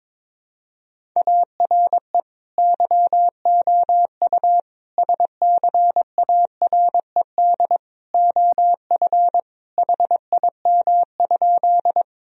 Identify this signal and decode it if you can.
Morse code, heard as a keyed tone, about 22 wpm, 700 Hz. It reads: ARE YOU SCARED OF HIM?
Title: Sentences from Top 1000 Words - Review Part 4 - 22wpm